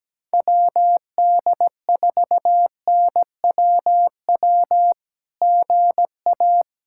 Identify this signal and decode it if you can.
Morse code, heard as a keyed tone, about 17 wpm, 700 Hz. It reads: WD4NWW GA
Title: SOTA QSO Practice 17wpm